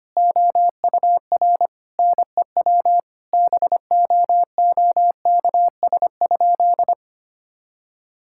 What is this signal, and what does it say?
Morse code, heard as a keyed tone, about 25 wpm, 700 Hz. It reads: OUR NEW BOOKS?
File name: practice / 25wpm / Sentences with 9-Word Phrases - 25wpm